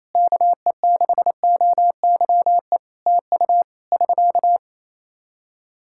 Morse code, heard as a keyed tone, about 28 wpm, 700 Hz. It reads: KE6OYE TU <SK>
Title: Practice QSO - Exchange 4 - Ending - 28wpm